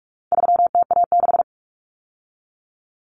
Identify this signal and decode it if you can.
Morse code, heard as a keyed tone, about 44 wpm, 700 Hz. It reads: 3TA6
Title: Common Cut Numbers - 4-Digit 45wpm